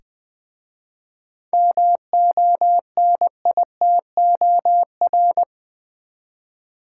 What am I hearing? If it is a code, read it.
Morse code, heard as a keyed tone, about 20 wpm, 700 Hz. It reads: MONITOR
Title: Easy Words with Hints - 20wpm